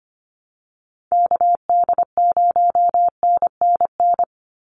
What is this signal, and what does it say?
Morse code, heard as a keyed tone, about 25 wpm, 700 Hz. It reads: KD0NNN
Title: USA Call Signs 25wpm